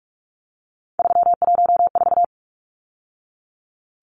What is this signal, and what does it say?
Morse code, heard as a keyed tone, about 44 wpm, 700 Hz. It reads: 314
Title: Numbers - 3-Digit 45wpm